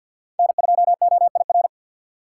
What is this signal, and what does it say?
Morse code, heard as a keyed tone, about 47 wpm, 700 Hz. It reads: N2OIR
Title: USA Call Signs - 5 Characters - 2x Character Spacing - 50wpm